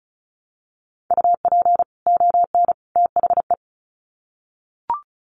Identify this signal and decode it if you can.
Morse code, heard as a keyed tone, about 34 wpm, 700 Hz. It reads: UP ON THE
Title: Sets of 3 Words 35wpm